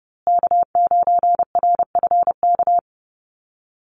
Morse code, heard as a keyed tone, about 30 wpm, 700 Hz. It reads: K9RFK
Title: USA Call Signs Repeated 3x 30wpm